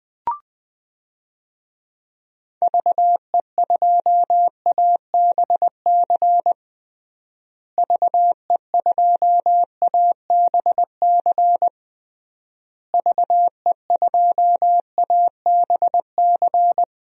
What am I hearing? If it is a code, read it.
Morse code, heard as a keyed tone, about 21 wpm, 700 Hz. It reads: VE2ABC VE2ABC VE2ABC
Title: DXCC-100 Call Signs - 6 Characters - Repeated 3x - 20wpm